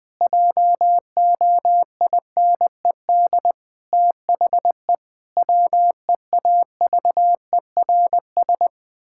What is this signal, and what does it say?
Morse code, heard as a keyed tone, about 20 wpm, 700 Hz. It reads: JOINED THE WEAVERS
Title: Pangram Challenge v2 - 20wpm